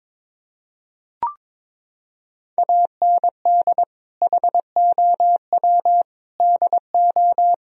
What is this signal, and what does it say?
Morse code, heard as a keyed tone, about 22 wpm, 700 Hz. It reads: AND HOW DO
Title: Sentences from Top 1000 Words - Review Part 4 - 22wpm